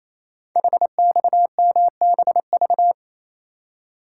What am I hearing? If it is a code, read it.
Morse code, heard as a keyed tone, about 29 wpm, 700 Hz. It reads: HXMBV
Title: Code Groups - 5 Letters - 28wpm